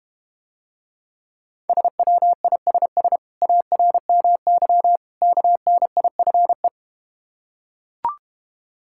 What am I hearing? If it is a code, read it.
Morse code, heard as a keyed tone, about 32 wpm, 700 Hz. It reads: SWISS ARMY KNIFE